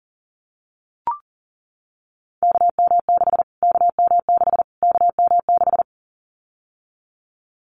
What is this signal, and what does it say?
Morse code, heard as a keyed tone, about 40 wpm, 700 Hz. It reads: KM6 KM6 KM6
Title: Callsign Prefixes for Top 25 DXCC - Repeated 3x - 40wpm